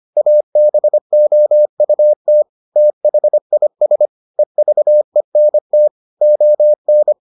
Morse code, heard as a keyed tone, about 25 wpm, 600 Hz. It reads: ABOUT THIS EVENT ON